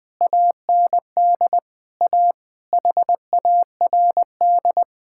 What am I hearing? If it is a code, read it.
Morse code, heard as a keyed tone, about 21 wpm, 700 Hz. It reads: AND A HARD
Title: Sentences with 8-Word Phrases - 20wpm